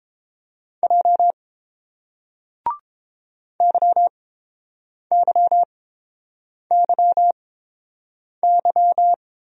Morse code, heard as a keyed tone, about 30 wpm, spaced as for 7 wpm, 700 Hz. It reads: J YYYY